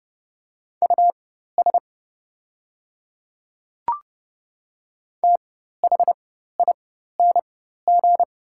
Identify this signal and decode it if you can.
Morse code, heard as a keyed tone, about 30 wpm, spaced as for 14 wpm, 690 Hz. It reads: US THING
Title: 4x Character Spacing - Top 500 Words - 30wpm